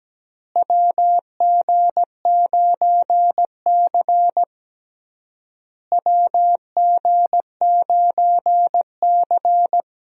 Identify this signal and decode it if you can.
Morse code, heard as a keyed tone, about 17 wpm, 700 Hz. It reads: WG9C WG9C
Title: USA Call Signs - 4 Characters - Repeated 3x - 17wpm